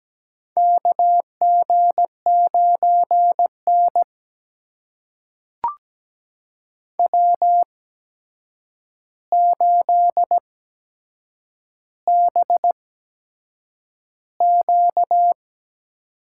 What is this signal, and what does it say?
Morse code, heard as a keyed tone, about 17 wpm, spaced as for 7 wpm, 700 Hz. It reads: KG9N W8BQ
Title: USA Call Signs - 4 Characters - 8x Character Spacing - 17wpm